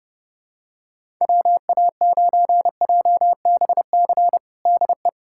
Morse code, heard as a keyed tone, about 30 wpm, 700 Hz. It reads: WA9JBC DE
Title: USA Callsign DE Callsign - 6 Characters - 30wpm